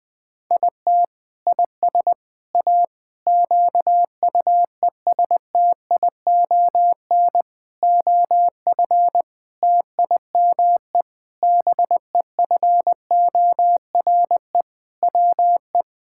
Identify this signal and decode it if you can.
Morse code, heard as a keyed tone, about 21 wpm, 700 Hz. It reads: IT IS A QUESTION OF TIME BEFORE WE
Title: Sentences from Top 1000 Words - Review Part 6 - 20wpm